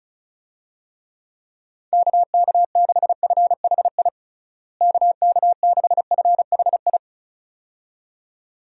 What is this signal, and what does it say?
Morse code, heard as a keyed tone, about 36 wpm, 700 Hz. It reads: KK6FHI KK6FHI
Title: USA Call Signs - 6 Characters - Repeated 2x - 35wpm